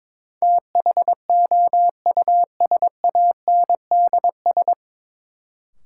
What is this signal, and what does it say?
Morse code, heard as a keyed tone, about 21 wpm, 700 Hz. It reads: THOUSANDS
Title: Sets of 4 Words - New Word Drill - 22wpm